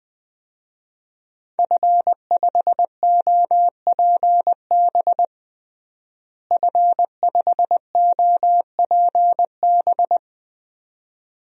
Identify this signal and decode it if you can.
Morse code, heard as a keyed tone, about 21 wpm, 700 Hz. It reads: F5OPB F5OPB
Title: DXCC-100 Call Signs - 5 Characters - Repeated 2x - 20wpm